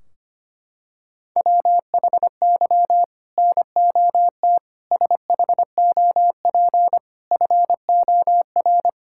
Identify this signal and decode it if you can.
Morse code, heard as a keyed tone, about 25 wpm, 700 Hz. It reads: WHY NOT SHOP FOR
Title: Sentences from Top 1600 Words 25wpm